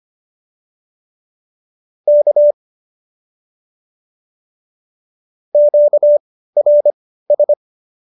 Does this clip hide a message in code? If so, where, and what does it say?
Yes: Morse code from 2.1 to 7.5 s, heard as a keyed tone, about 25 wpm, spaced as for 15 wpm, 590 Hz. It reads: K QRS